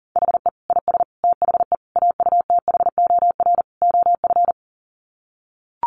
Morse code, heard as a keyed tone, about 40 wpm, 700 Hz. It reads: HE IS THE AUTHOR OF
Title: Sets of 5 Words Encore2 40wpm